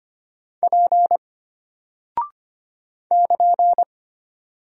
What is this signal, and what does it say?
Morse code, heard as a keyed tone, about 25 wpm, 700 Hz. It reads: P <KN>
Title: Single Character (Letter:Number:Punctuation:Prosign) Alt 25wpm